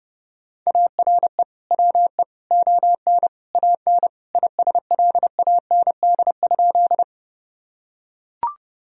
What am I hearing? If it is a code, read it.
Morse code, heard as a keyed tone, about 30 wpm, 700 Hz. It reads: ARE WE ON AN ISLAND?
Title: Sentences from Top 1000 Words - Review Part 3 - 30wpm